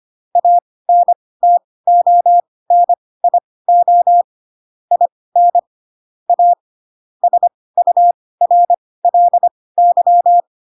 Morse code, heard as a keyed tone, about 25 wpm, spaced as for 18 wpm, 700 Hz. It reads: ANTONIO IN A SURLY
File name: Pinocchio / Pinocchio ch2 12 25 ik1hge